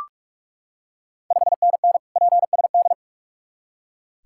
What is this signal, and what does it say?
Morse code, heard as a keyed tone, about 41 wpm, 690 Hz. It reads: 5NN PSD
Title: Ontario QSO Party Exchange 45wpm